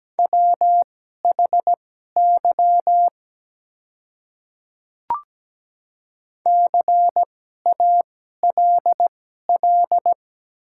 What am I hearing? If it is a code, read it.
Morse code, heard as a keyed tone, about 17 wpm, spaced as for 12 wpm, 700 Hz. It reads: WHY CALL
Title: 2x Character Spacing - Top 200 Words - 17wpm